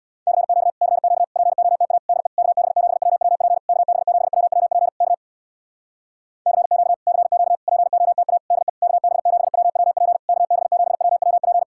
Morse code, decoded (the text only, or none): CQ CQ POTA DE KB7WWP KB7WWP K CQ CQ POTA DE KB7WWP KB7WWP